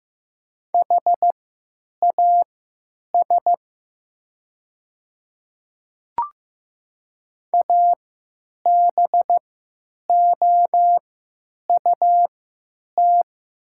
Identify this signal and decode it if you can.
Morse code, heard as a keyed tone, about 15 wpm, spaced as for 9 wpm, 700 Hz. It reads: HAS ABOUT
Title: 3x Character Spacing - Top 100 Words - 15wpm